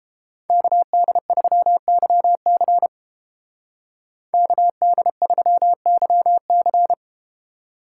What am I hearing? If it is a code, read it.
Morse code, heard as a keyed tone, about 30 wpm, 700 Hz. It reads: KD3YC KD3YC